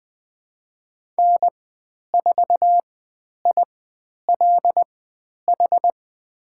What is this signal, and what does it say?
Morse code, heard as a keyed tone, about 20 wpm, spaced as for 10 wpm, 700 Hz. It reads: N4ILH
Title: 095 Lesson 16g - Letter c - Call Signs 20wpm